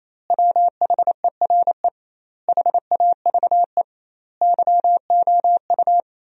Morse code, heard as a keyed tone, about 28 wpm, 700 Hz. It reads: WHERE HAVE YOU
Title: 2x Word-Spacing - Sentences from Top 400 Words - 28wpm